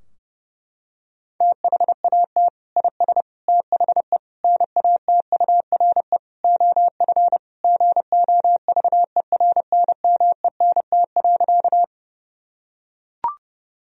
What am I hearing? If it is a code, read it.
Morse code, heard as a keyed tone, about 30 wpm, 700 Hz. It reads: THAT IS THE NATURE OF GOVERNMENT.
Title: Sentences from Top 1000 Words - Review Part 3 - 30wpm